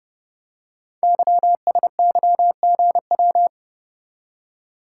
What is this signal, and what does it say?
Morse code, heard as a keyed tone, about 30 wpm, 700 Hz. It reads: YSYGW
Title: Code Groups - 5 Letters - 30wpm